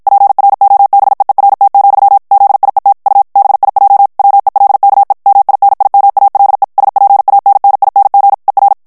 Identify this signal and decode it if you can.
Morse code, heard as a keyed tone, about 52 wpm, 800 Hz. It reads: PROBEERT, ZIET U BIJ WELKE MINIMALE SPANNING ER
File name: HOLLANDAIS texte clair 2 vitesse 50 mots minute